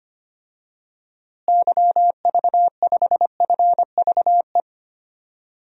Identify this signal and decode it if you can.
Morse code, heard as a keyed tone, about 25 wpm, 700 Hz. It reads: YV5FVE
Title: DXCC-100 Call Signs - 6 Characters - Repeated 3x - 25wpm